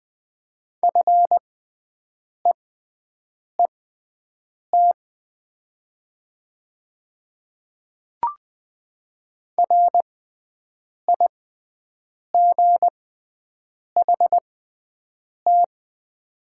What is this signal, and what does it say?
Morse code, heard as a keyed tone, about 20 wpm, spaced as for 7 wpm, 700 Hz. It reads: FEET RIGHT
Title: 6x Character Spacing - Top 300 Words - 20wpm